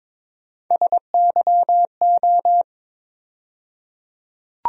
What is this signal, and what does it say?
Morse code, heard as a keyed tone, about 21 wpm, 700 Hz. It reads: SYO
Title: USA Call Sign Postfixes - 3 Letters 22wpm